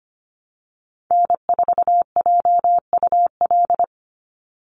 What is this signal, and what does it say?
Morse code, heard as a keyed tone, about 25 wpm, 700 Hz. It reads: N4JUL
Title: USA Call Signs 25wpm